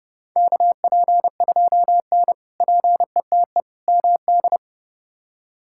Morse code, heard as a keyed tone, about 32 wpm, 700 Hz. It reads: KP2N PETE MD
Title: K1USN SST QSO Practice - 30wpm